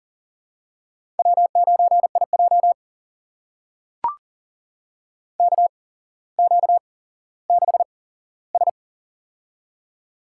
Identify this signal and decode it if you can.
Morse code, heard as a keyed tone, about 38 wpm, spaced as for 17 wpm, 700 Hz. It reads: W9IJ KQ6S
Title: USA Call Signs - 4 Characters - 8x Character Spacing - 40wpm